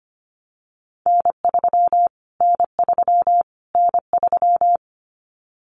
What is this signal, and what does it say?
Morse code, heard as a keyed tone, about 25 wpm, 700 Hz. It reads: N3 N3 N3
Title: Callsign Prefixes for Top 25 DXCC - Repeated 3x - 25wpm